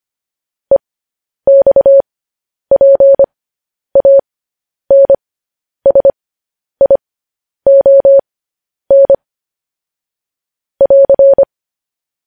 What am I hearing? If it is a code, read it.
Morse code, heard as a keyed tone, about 25 wpm, spaced as for 10 wpm, 550 Hz. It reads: EXPANSION <AR>
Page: news headlines in morse code at 10 wpm